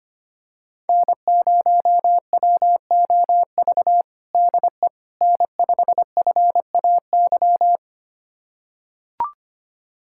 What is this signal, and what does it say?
Morse code, heard as a keyed tone, about 25 wpm, 700 Hz. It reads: N0WOV DE N5FAY
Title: USA Callsign DE Callsign - 5 Characters - 25wpm